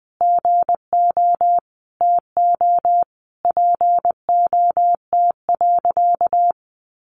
Morse code, heard as a keyed tone, about 20 wpm, 700 Hz. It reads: GO TO POT.
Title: Sentences from Top 1000 Words 20wpm